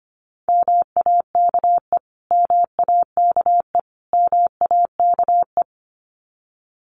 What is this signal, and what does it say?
Morse code, heard as a keyed tone, about 25 wpm, 700 Hz. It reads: MAKE MAKE MAKE
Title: Top 200 Words Repeated 25wpm